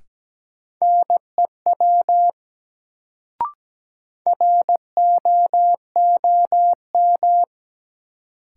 Morse code, heard as a keyed tone, about 17 wpm, 700 Hz. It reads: NEW ROOM
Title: Sentences from Top 300 Words - New Word Drill - 17wpm